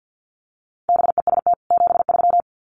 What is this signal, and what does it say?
Morse code, heard as a keyed tone, about 50 wpm, 700 Hz. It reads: BEST 73
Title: Common QSO Phrases 50wpm